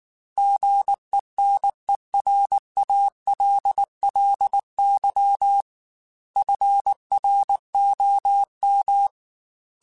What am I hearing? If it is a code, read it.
Morse code, heard as a keyed tone, about 19 wpm, 800 Hz. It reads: GENERALLY FROM
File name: Weather Sets / Weather Sets-mp3s / Weather 4 10 @18